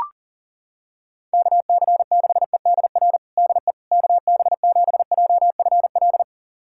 Morse code, heard as a keyed tone, about 40 wpm, 700 Hz. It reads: KC6EDR DE KB7JFL
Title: USA Callsign DE Callsign 40wpm